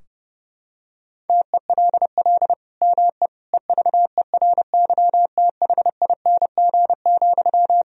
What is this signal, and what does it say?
Morse code, heard as a keyed tone, about 30 wpm, 700 Hz. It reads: TELL ME EVERYTHING,
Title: Sentences from Top 1000 Words - Review Part 4 - 30wpm